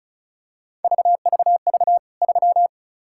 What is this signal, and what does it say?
Morse code, heard as a keyed tone, about 34 wpm, 700 Hz. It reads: VVV 3
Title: K1USN SST QSO Practice 35wpm